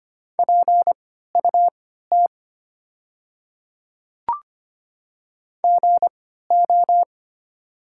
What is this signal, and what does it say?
Morse code, heard as a keyed tone, about 25 wpm, spaced as for 14 wpm, 700 Hz. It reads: PUT GO
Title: 3x Character Spacing - Top 200 Words - 25wpm